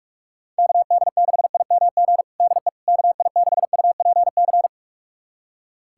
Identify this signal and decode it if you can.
Morse code, heard as a keyed tone, about 46 wpm, 700 Hz. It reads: KD6IMG DE KI6UPC
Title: USA Callsign DE Callsign 45wpm